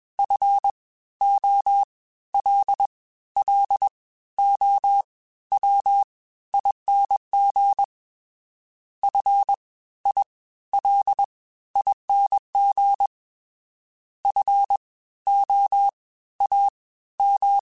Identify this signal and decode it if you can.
Morse code, heard as a keyed tone, about 21 wpm, spaced as for 14 wpm, 800 Hz. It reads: FOLLOWING FILING FOAM